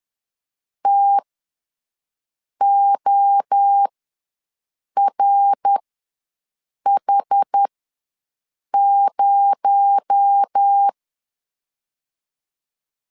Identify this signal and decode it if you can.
Morse code, heard as a keyed tone, about 10 wpm, spaced as for 5 wpm, 800 Hz. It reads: TORH0
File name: LEZIONI / 6 PKX0 / 6 PKX0 ws 10wpm 4220